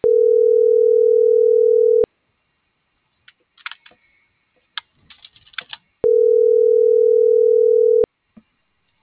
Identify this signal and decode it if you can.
Ringback tone (North America), 440 + 480 Hz.